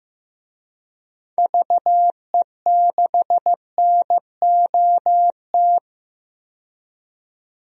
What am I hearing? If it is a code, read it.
Morse code, heard as a keyed tone, about 15 wpm, 700 Hz. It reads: VE6NOT
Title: DXCC-100 Call Signs - 6 Characters - 15wpm